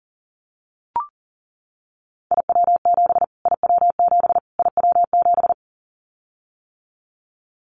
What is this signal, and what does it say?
Morse code, heard as a keyed tone, about 40 wpm, 700 Hz. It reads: IW7 IW7 IW7
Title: Callsign Prefixes for Top 25 DXCC - Repeated 3x - 40wpm